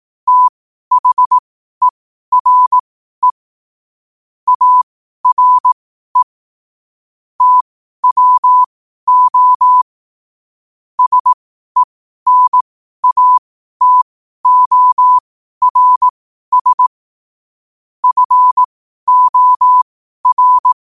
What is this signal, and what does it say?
Morse code, heard as a keyed tone, about 18 wpm, spaced as for 12 wpm, 1000 Hz. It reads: THERE ARE TWO SENATORS FOR
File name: qotdfiles / 12wpm